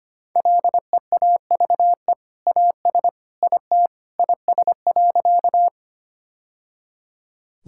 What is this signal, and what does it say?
Morse code, heard as a keyed tone, about 25 wpm, 690 Hz. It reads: LEAVE AS IT IS.
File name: practice / 25wpm / Sentences with 2-Word Phrases - 25wpm